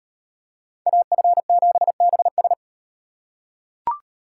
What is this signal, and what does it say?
Morse code, heard as a keyed tone, about 38 wpm, 700 Hz. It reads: AF7BS